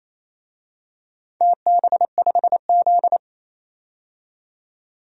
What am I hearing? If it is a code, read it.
Morse code, heard as a keyed tone, about 28 wpm, 700 Hz. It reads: TB5Z
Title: DXCC-100 Call Signs - 4 Characters - 28wpm